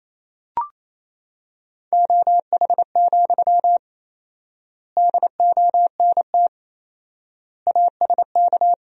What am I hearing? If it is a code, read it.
Morse code, heard as a keyed tone, about 28 wpm, 700 Hz. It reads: OH, DONT ASK
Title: 4x Word-Spacing - Sentences from Top 500 Words - 28wpm